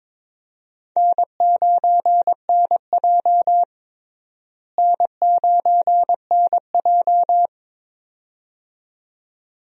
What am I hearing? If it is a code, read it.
Morse code, heard as a keyed tone, about 22 wpm, 700 Hz. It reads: N9NJ N9NJ
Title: USA Call Signs - 4 Characters - Repeated 2x - 22wpm